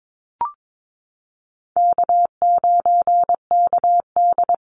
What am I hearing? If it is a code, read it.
Morse code, heard as a keyed tone, about 22 wpm, 700 Hz. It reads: K9KD
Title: USA Call Signs Repeated 3x 22wpm